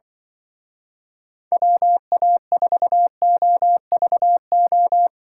Morse code, heard as a keyed tone, about 25 wpm, 700 Hz. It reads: WA4OVO
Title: Speed Racing - USA 2x3 Call Signs 20wpm